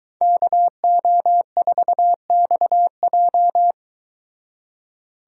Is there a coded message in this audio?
Yes: Morse code from 0.2 to 3.7 s, heard as a keyed tone, about 23 wpm, 700 Hz. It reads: KO4XJ